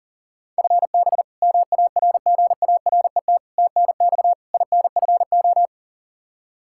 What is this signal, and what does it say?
Morse code, heard as a keyed tone, about 40 wpm, 700 Hz. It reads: FB MARGARET TNX INFO
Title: Practice QSO - Exchange 3 - More Info - 40wpm